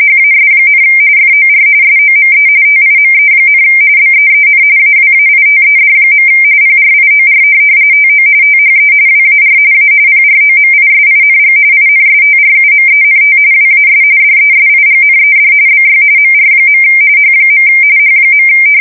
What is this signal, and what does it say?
RTTY, about 45.45 baud. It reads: HAS, FOR SOME TIME, BEEN AWAITED. PRIOR TO THE WAR A PROMISING INSTRUMENT WAS DESIGNED BY MR. H. H.
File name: fsk170 / morkrum